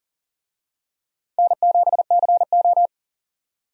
Morse code, heard as a keyed tone, about 40 wpm, 700 Hz. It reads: N7CO
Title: USA Call Signs - 4 Characters - 40wpm